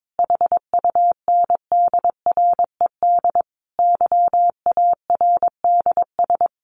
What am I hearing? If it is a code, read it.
Morse code, heard as a keyed tone, about 22 wpm, 700 Hz. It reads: HUNDRED YARDS
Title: A Sound Of Thunder 22wpm